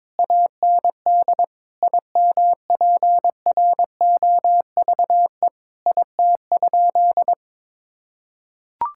Morse code, heard as a keyed tone, about 22 wpm, 700 Hz. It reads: AND IMPROVE IT?
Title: Sentences from Top 1000 Words - Review Part 5 - 22wpm